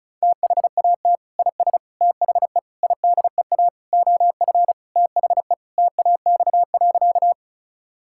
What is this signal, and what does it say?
Morse code, heard as a keyed tone, about 36 wpm, 700 Hz. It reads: THAT IS THE IDEA OF THE TAX.
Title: Sentences from Top 1000 Words - Review Part 5 - 35wpm